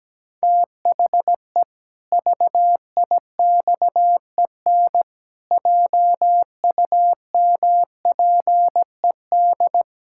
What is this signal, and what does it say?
Morse code, heard as a keyed tone, about 17 wpm, 700 Hz. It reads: THE VIXEN JUMPED
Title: Pangram Challenge v2 - 17wpm